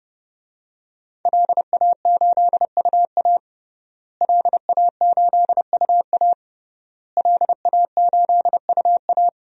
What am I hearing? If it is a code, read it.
Morse code, heard as a keyed tone, about 32 wpm, 700 Hz. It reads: LA8UA LA8UA LA8UA
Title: DXCC-100 Call Signs - 5 Characters - Repeated 3x - 30wpm